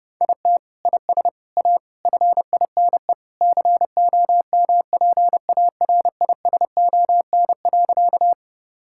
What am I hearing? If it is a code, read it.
Morse code, heard as a keyed tone, about 32 wpm, 700 Hz. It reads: IT IS A FINE COMPARISON.
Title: Sentences from Top 1600 Words 30wpm